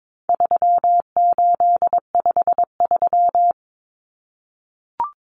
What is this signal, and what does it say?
Morse code, heard as a keyed tone, about 22 wpm, 700 Hz. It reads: 3853
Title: Common Cut Numbers - 4-Digit 22wpm